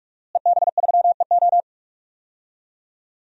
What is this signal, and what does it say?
Morse code, heard as a keyed tone, about 44 wpm, 700 Hz. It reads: EB3EO
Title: DXCC-100 Call Signs - 5 Characters - Repeated 2x - 45wpm